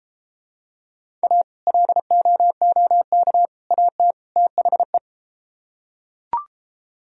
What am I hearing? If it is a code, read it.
Morse code, heard as a keyed tone, about 33 wpm, 700 Hz. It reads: A LOOK AT THE